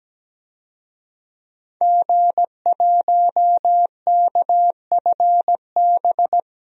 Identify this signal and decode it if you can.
Morse code, heard as a keyed tone, about 17 wpm, 700 Hz. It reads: G1KFB
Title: DXCC-100 Call Signs - 5 Characters - 17wpm